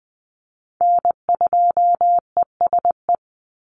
Morse code, heard as a keyed tone, about 19 wpm, 700 Hz. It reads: N2ESE
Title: NA SOTA Chaser Callsigns 20wpm